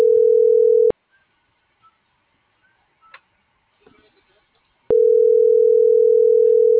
Ringback tone (North America), 440 + 480 Hz.